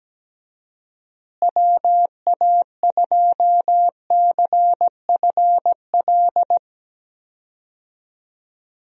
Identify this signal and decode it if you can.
Morse code, heard as a keyed tone, about 17 wpm, 700 Hz. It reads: WA2CFL
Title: USA Call Signs - 6 Characters - Repeated 3x - 17wpm